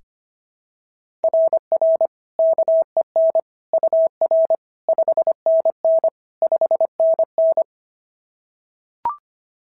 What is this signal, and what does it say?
Morse code, heard as a keyed tone, about 25 wpm, 690 Hz. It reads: RR KEN UR 5NN 5NN
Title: SOTA QSO Practice 25wpm